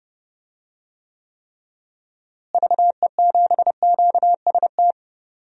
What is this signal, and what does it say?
Morse code, heard as a keyed tone, about 30 wpm, 700 Hz. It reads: VE7QST
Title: DXCC-100 Call Signs - 6 Characters - Repeated 3x - 30wpm